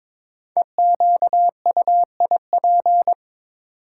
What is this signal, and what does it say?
Morse code, heard as a keyed tone, about 22 wpm, 700 Hz. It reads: EQUIP
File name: 5 Letter Words - First 3 Letters Given - 22wpm